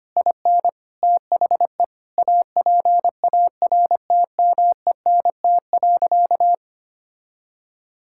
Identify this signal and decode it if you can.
Morse code, heard as a keyed tone, about 25 wpm, 700 Hz. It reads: IN THE APARTMENT.
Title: Sentences from Top 1000 Words - Review Part 6 - 25wpm